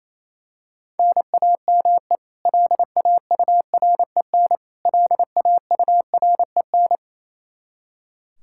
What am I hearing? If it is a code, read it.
Morse code, heard as a keyed tone, about 28 wpm, 700 Hz. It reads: NAME LAUREN LAUREN
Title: Practice QSO - Exchange 1 - Beginning - 28wpm